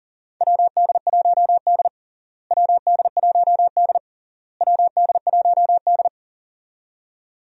Morse code, heard as a keyed tone, about 40 wpm, 700 Hz. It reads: WD1D WD1D WD1D
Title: USA Call Signs - 4 Characters - Repeated 3x - 40wpm